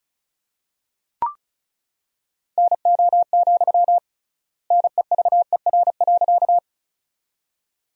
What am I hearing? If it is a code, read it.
Morse code, heard as a keyed tone, about 36 wpm, 700 Hz. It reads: NO, NEVER.
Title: 3x Word-Spacing - Sentences from Top 300 Words - 35wpm